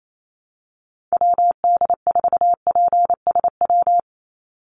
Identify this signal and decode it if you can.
Morse code, heard as a keyed tone, about 28 wpm, 700 Hz. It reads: WD4PSW
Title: USA Call Signs 28wpm